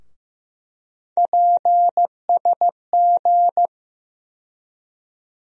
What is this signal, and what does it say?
Morse code, heard as a keyed tone, about 15 wpm, 700 Hz. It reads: PSG
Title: USA Call Sign Postfixes - 3 Letters 15wpm